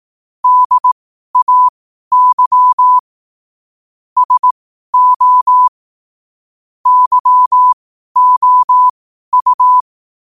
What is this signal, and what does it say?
Morse code, heard as a keyed tone, about 18 wpm, spaced as for 12 wpm, 1000 Hz. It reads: DAY SO YOU
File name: qotdfiles / 12wpm